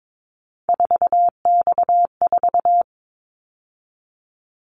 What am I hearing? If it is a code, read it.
Morse code, heard as a keyed tone, about 22 wpm, 700 Hz. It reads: 4X4
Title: Callsign Prefixes for Top 50 DXCC - 22wpm